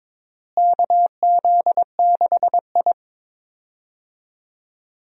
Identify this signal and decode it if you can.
Morse code, heard as a keyed tone, about 21 wpm, 700 Hz. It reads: KZ6I
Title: USA Call Signs - 4 Characters - Repeated 3x - 22wpm